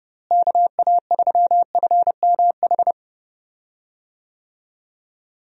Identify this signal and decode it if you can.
Morse code, heard as a keyed tone, about 30 wpm, 700 Hz. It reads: KA3FMH
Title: USA Call Signs - 6 Characters - Repeated 2x - 30wpm